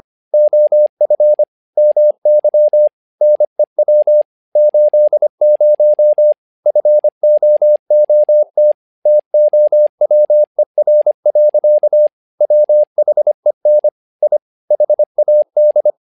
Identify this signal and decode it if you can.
Morse code, heard as a keyed tone, about 25 wpm, 600 Hz. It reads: OF MY NEW 80 FOOT TOWER. WHEN I HAD
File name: learning center / tower trouble / tower trouble 25wpm